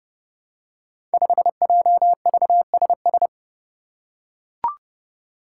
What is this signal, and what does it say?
Morse code, heard as a keyed tone, about 30 wpm, 700 Hz. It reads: 5JVSS
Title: Five Challenging Letter-Number Combinations 30wpm